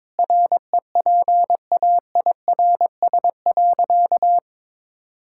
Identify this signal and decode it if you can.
Morse code, heard as a keyed tone, about 22 wpm, 700 Hz. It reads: REPAIRS.
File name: Difficult Sentences with Hints - 22wpm